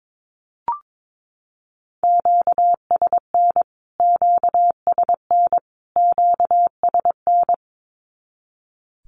Morse code, heard as a keyed tone, about 22 wpm, 700 Hz. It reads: QSN QSN QSN
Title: Q-Codes Repeated 3x 22wpm